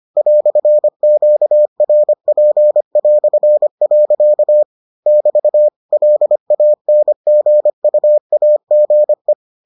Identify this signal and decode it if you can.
Morse code, heard as a keyed tone, about 25 wpm, 600 Hz. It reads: "QRP". <BT> LANGUAGE